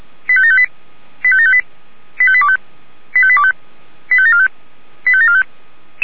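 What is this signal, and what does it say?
Five-tone selective call (ZVEI), heard as quick runs of five tones, tones of about 70 ms. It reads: 85758 85758 85825 85825 85745 85745
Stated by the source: ZVEI five-tone selective call 85827 received on an SDR.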